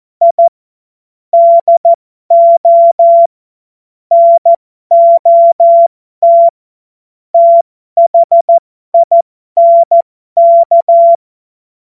Morse code, heard as a keyed tone, about 14 wpm, spaced as for 12 wpm, 680 Hz. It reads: I DO NOT THINK
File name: Morsekurs / Witze